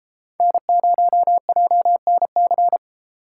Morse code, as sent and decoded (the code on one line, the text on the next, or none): -. ----- .--- -. -.-.
N0JNC